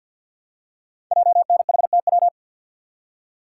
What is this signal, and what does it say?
Morse code, heard as a keyed tone, about 48 wpm, 700 Hz. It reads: JNHTW
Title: Code Groups - 5 Letters - 50wpm